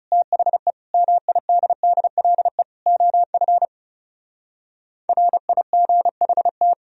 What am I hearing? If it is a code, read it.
Morse code, heard as a keyed tone, about 38 wpm, 700 Hz. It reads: THE MIDDLE OF RIGHT